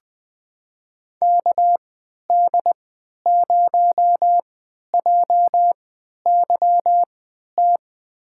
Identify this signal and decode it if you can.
Morse code, heard as a keyed tone, about 20 wpm, spaced as for 11 wpm, 700 Hz. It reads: KD0JYT
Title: USA Call Signs - 6 Characters - 3x Character Spacing - 20wpm